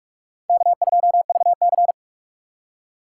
Morse code, heard as a keyed tone, about 40 wpm, 700 Hz. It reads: K2VC